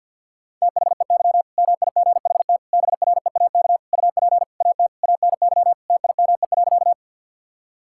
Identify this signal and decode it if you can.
Morse code, heard as a keyed tone, about 50 wpm, 700 Hz. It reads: THEY MIGHT BREAK UP AT ANY TIME.
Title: Sentences from Top 1000 Words - Review Part 6 - 50wpm